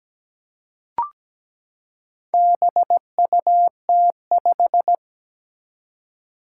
Morse code, heard as a keyed tone, about 17 wpm, 700 Hz. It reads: BUT5
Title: Four Challenging Letter-Number Combinations 17wpm